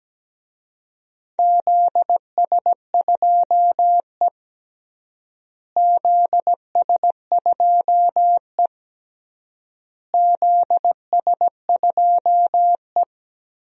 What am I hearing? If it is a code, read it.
Morse code, heard as a keyed tone, about 17 wpm, 700 Hz. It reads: ZS2E ZS2E ZS2E
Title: DXCC-100 Call Signs - 4 Characters - Repeated 3x - 17wpm